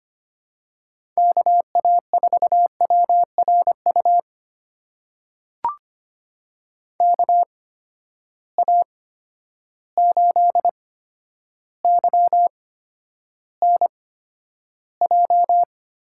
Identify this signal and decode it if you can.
Morse code, heard as a keyed tone, about 25 wpm, spaced as for 11 wpm, 700 Hz. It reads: KA4WRU KA8YNJ
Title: USA Call Signs - 6 Characters - 8x Character Spacing - 25wpm